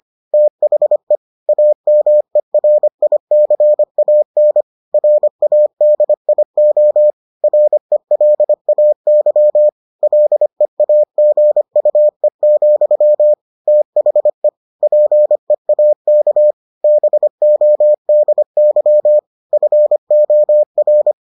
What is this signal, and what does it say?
Morse code, heard as a keyed tone, about 25 wpm, 600 Hz. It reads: THE AMERICAN RADIO RELAY LEAGUE, THE PEAK BODY FOR